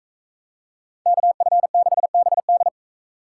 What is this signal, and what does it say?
Morse code, heard as a keyed tone, about 44 wpm, 700 Hz. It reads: KF6BD